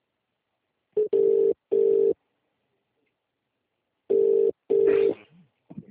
Ringback tone (UK), 400 + 450 Hz.